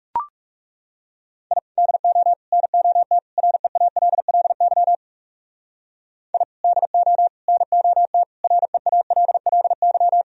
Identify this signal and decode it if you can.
Morse code, heard as a keyed tone, about 35 wpm, 700 Hz. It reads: I DO NOT REALLY I DO NOT REALLY
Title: Speed-Racing - Sets of 4 Words 30wpm